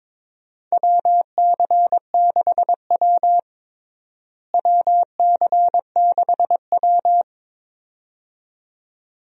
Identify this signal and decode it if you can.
Morse code, heard as a keyed tone, about 22 wpm, 700 Hz. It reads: WC6W WC6W
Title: USA Call Signs - 4 Characters - Repeated 2x - 22wpm